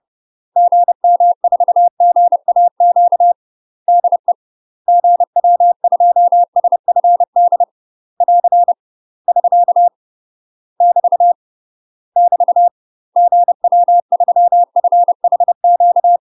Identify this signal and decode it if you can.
Morse code, heard as a keyed tone, about 30 wpm, 700 Hz. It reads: GM4GAQ DE GW2SFD <AR> <SK> <BT> <BT> GW3FHQ
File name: UK MASTER QSOs TRAINING / 30WPM / UK MASTER QSOs 0019